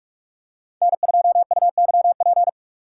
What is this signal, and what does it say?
Morse code, heard as a keyed tone, about 46 wpm, 700 Hz. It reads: N2UYP
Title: USA Call Signs - 5 Characters - 6x Character Spacing - 45wpm